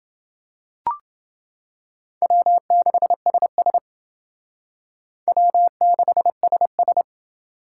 Morse code, heard as a keyed tone, about 28 wpm, 700 Hz. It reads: W6SS W6SS